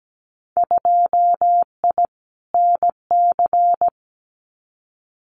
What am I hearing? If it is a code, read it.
Morse code, heard as a keyed tone, about 17 wpm, 700 Hz. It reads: 2I NC
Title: Winter Field Day Exchanges 17wpm